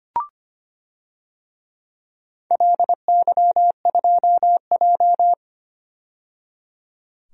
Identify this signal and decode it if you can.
Morse code, heard as a keyed tone, about 25 wpm, 700 Hz. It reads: LY2J
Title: DXCC-100 Call Signs - 4 Characters - 25wpm